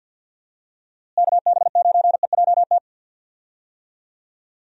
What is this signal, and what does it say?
Morse code, heard as a keyed tone, about 49 wpm, 700 Hz. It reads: KB9EJT
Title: USA Call Signs - 6 Characters - Repeated 3x - 50wpm